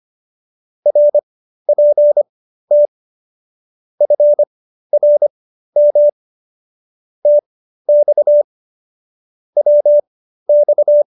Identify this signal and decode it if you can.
Morse code, heard as a keyed tone, about 25 wpm, spaced as for 13 wpm, 590 Hz. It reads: RPT FRM TX WX